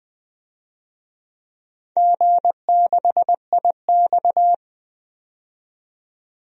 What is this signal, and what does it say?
Morse code, heard as a keyed tone, about 20 wpm, 700 Hz. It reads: G6IX
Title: DXCC-100 Call Signs - 4 Characters - 20wpm